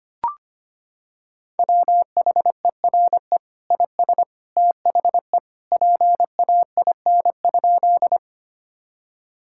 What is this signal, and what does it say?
Morse code, heard as a keyed tone, about 25 wpm, 700 Hz. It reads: WHERE IS THE PAIN?
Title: Sentences from Top 1600 Words 25wpm